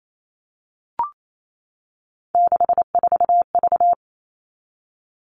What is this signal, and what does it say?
Morse code, heard as a keyed tone, about 28 wpm, 700 Hz. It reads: 64V
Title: All Cut Numbers - 3-Digit 28wpm